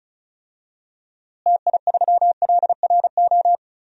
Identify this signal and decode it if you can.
Morse code, heard as a keyed tone, about 36 wpm, 700 Hz. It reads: TI3LRO
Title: DXCC-100 Call Signs - 6 Characters - Repeated 2x - 35wpm